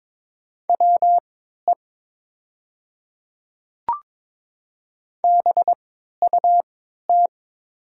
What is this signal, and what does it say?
Morse code, heard as a keyed tone, about 22 wpm, spaced as for 13 wpm, 700 Hz. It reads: WE BUT
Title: 3x Character Spacing - Top 100 Words - 22wpm